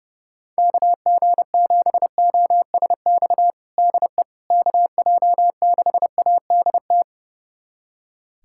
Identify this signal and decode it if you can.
Morse code, heard as a keyed tone, about 30 wpm, 700 Hz. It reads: KG7OSX DE KJ6ADT
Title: USA Callsign DE Callsign 30wpm